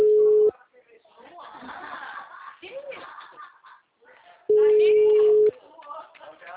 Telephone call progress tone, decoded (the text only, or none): ringback tone